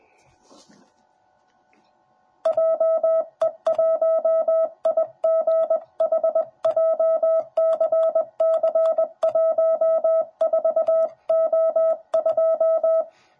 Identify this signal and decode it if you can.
Morse code, heard as a keyed tone, about 20 wpm, 650 Hz. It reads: JE1IGH JCC14O2
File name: jh8cbh / 2021cw / cw10b